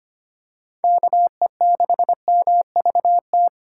Morse code, heard as a keyed tone, about 25 wpm, 700 Hz. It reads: KE6MVT
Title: USA Call Signs - 6 Characters - 2x Character Spacing - 25wpm